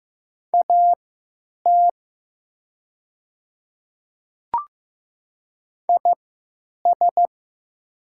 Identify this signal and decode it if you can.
Morse code, heard as a keyed tone, about 15 wpm, spaced as for 8 wpm, 700 Hz. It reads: AT IS
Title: 3x Character Spacing - Top 100 Words - 15wpm